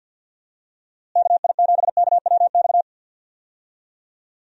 Morse code, heard as a keyed tone, about 50 wpm, 700 Hz. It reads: KI7KWX
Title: USA Call Signs - 6 Characters - Repeated 3x - 50wpm